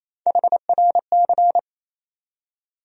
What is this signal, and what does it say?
Morse code, heard as a keyed tone, about 28 wpm, 700 Hz. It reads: HRC
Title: USA Call Sign Postfixes - 3 Letters 28wpm